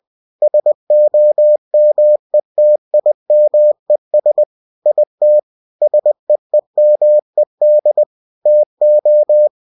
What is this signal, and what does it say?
Morse code, heard as a keyed tone, about 20 wpm, 600 Hz. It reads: SOMETIMES IT SEEMED TO